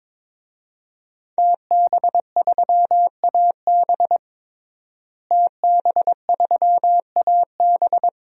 Morse code, heard as a keyed tone, about 22 wpm, 700 Hz. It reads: TB3AB TB3AB
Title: DXCC-100 Call Signs - 5 Characters - Repeated 3x - 22wpm